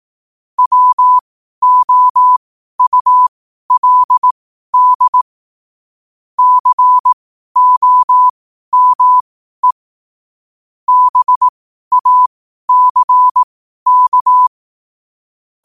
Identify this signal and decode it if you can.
Morse code, heard as a keyed tone, about 18 wpm, spaced as for 12 wpm, 1000 Hz. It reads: WOULD COME BACK
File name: qotdfiles / 12wpm